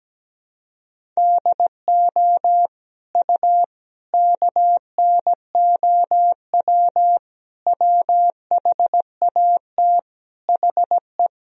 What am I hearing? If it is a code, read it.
Morse code, heard as a keyed tone, about 17 wpm, 700 Hz. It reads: DO U KNOW WHAT HE
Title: Sentences from Top 300 Words and Abbreviations 17wpm